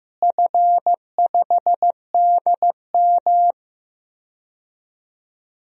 Morse code, heard as a keyed tone, about 15 wpm, 700 Hz. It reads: F5DM
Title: DXCC-100 Call Signs - 4 Characters - Repeated 3x - 15wpm